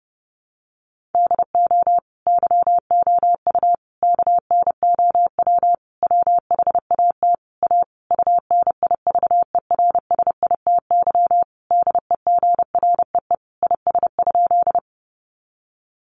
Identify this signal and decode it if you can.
Morse code, heard as a keyed tone, about 30 wpm, 700 Hz. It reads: DO YOU KNOW WHAT A UNIVERSITY DEGREE IS?
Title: Sentences from Top 1500 Words 30wpm